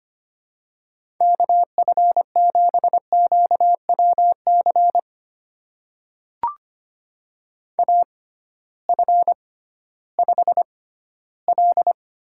Morse code, heard as a keyed tone, about 25 wpm, spaced as for 15 wpm, 700 Hz. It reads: KF7QWC AF5L
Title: USA Call Signs - 6x Character Spacing - 25wpm